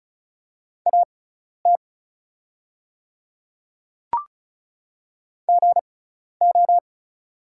Morse code, heard as a keyed tone, about 34 wpm, spaced as for 12 wpm, 700 Hz. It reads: AT GO
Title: 6x Character Spacing - Top 300 Words - 35wpm